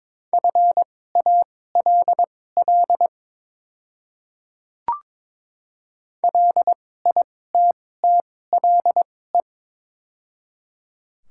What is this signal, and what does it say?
Morse code, heard as a keyed tone, about 22 wpm, spaced as for 16 wpm, 700 Hz. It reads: FALL LITTLE
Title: 2x Character Spacing - Top 500 Words - 22wpm